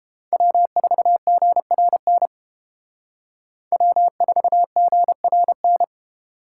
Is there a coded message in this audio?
Yes: Morse code from 0.3 to 5.8 s, heard as a keyed tone, about 30 wpm, 700 Hz. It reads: W4GRN W4GRN